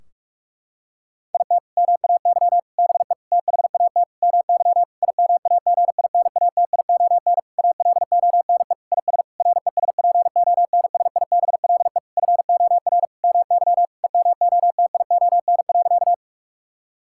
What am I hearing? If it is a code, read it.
Morse code, heard as a keyed tone, about 46 wpm, 700 Hz. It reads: IT MAY BE THAT MY IMAGINATION ALONE IS RESPONSIBLE FOR MY EMOTION.
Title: Sentences from Top 1600 Words 45wpm